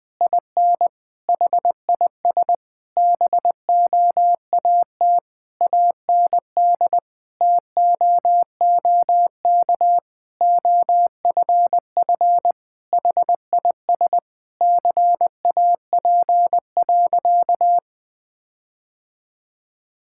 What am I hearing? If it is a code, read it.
Morse code, heard as a keyed tone, about 20 wpm, 700 Hz. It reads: IN HIS BOAT AND TOOK OFF HIS CAP.
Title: Sentences from Top 1600 Words 20wpm